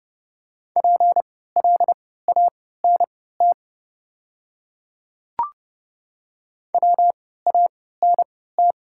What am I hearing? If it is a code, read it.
Morse code, heard as a keyed tone, about 31 wpm, spaced as for 17 wpm, 700 Hz. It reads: PLANT WANT
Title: 3x Character Spacing - Top 500 Words - 30wpm